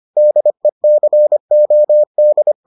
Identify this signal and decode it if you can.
Morse code, heard as a keyed tone, about 25 wpm, 600 Hz. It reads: DECOD